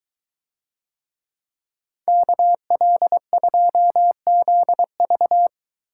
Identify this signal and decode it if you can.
Morse code, heard as a keyed tone, about 22 wpm, 700 Hz. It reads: KL2ZV